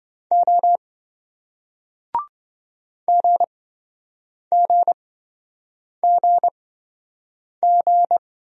Morse code, heard as a keyed tone, about 28 wpm, spaced as for 7 wpm, 700 Hz. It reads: O GGGG